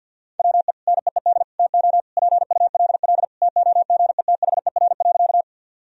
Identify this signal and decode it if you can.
Morse code, heard as a keyed tone, about 50 wpm, 700 Hz. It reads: WE NEED TO PULL TOGETHER.
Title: Sentences from Top 1000 Words - Review Part 2 - 50wpm